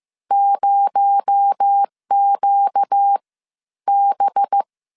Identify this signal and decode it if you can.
Morse code, heard as a keyed tone, about 15 wpm, 800 Hz. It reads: 0Q B
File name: CW / DET / 1 6 1500C S C 14wpm 2951